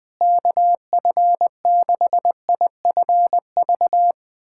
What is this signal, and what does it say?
Morse code, heard as a keyed tone, about 20 wpm, 700 Hz. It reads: KF6IFV
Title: USA Callsign DE Callsign - 6 Characters - 20wpm